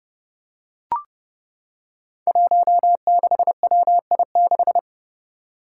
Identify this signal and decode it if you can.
Morse code, heard as a keyed tone, about 30 wpm, 700 Hz. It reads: 16WI6
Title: Five Challenging Letter-Number Combinations 30wpm